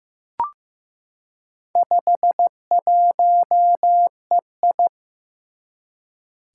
Morse code, heard as a keyed tone, about 15 wpm, 700 Hz. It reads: 51EI
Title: Four Challenging Letter-Number Combinations 15wpm